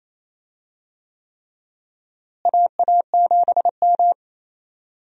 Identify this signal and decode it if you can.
Morse code, heard as a keyed tone, about 28 wpm, 700 Hz. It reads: AA7M
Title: USA Call Signs - 4 Characters - Repeated 3x - 28wpm